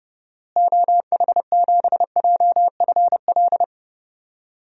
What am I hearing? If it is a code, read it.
Morse code, heard as a keyed tone, about 32 wpm, 700 Hz. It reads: OH7JFL
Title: DXCC-100 Call Signs - 6 Characters - Repeated 3x - 30wpm